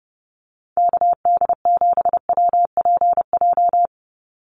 Morse code, heard as a keyed tone, about 29 wpm, 700 Hz. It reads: KD7WPJ
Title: NA SOTA Activator Callsigns 30wpm